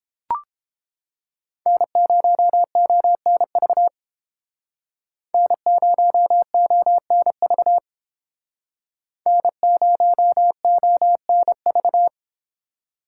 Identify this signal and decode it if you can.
Morse code, heard as a keyed tone, about 29 wpm, 700 Hz. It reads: N0ONV N0ONV N0ONV